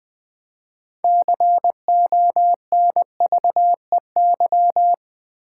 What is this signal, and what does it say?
Morse code, heard as a keyed tone, about 20 wpm, 700 Hz. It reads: CONVEY
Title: Build A Word - 2 Letters At A Time - 6-Letter Words - 20wpm